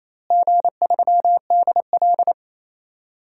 Morse code, heard as a keyed tone, about 28 wpm, 700 Hz. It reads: G3DL
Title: DXCC-100 Call Signs - 4 Characters - Repeated 3x - 28wpm